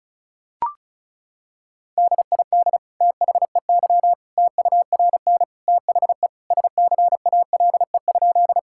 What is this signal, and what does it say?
Morse code, heard as a keyed tone, about 34 wpm, 700 Hz. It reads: DID THEY TURN THE SCALE?
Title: Sentences from Top 1000 Words - Review Part 5 - 35wpm